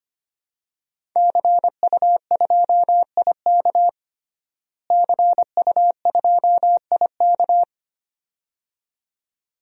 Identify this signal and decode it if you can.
Morse code, heard as a keyed tone, about 25 wpm, 700 Hz. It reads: CU2IK CU2IK
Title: DXCC-100 Call Signs - 5 Characters - Repeated 2x - 25wpm